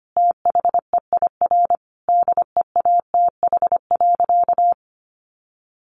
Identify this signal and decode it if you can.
Morse code, heard as a keyed tone, about 25 wpm, 700 Hz. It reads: THEIR DEATH.
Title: Sentences from Top 1100 Words 25wpm